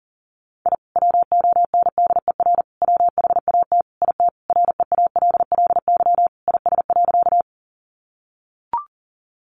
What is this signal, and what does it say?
Morse code, heard as a keyed tone, about 40 wpm, 700 Hz. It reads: I WONDER WHAT IT REALLY IS.
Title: Sentences from Top 600 Words 40wpm